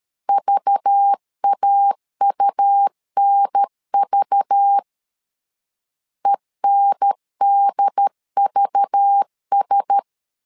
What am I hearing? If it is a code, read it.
Morse code, heard as a keyed tone, about 13 wpm, 800 Hz. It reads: VAUNV ENDVS
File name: LEZIONI / 5 LBVC / 5 LBV 12wpm 1855